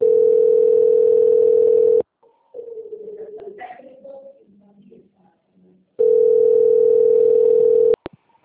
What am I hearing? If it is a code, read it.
Ringback tone (North America), 440 + 480 Hz.